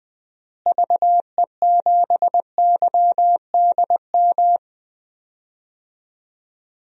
Morse code, heard as a keyed tone, about 20 wpm, 700 Hz. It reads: VE7YDM
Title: DXCC-100 Call Signs - 6 Characters - Repeated 2x - 20wpm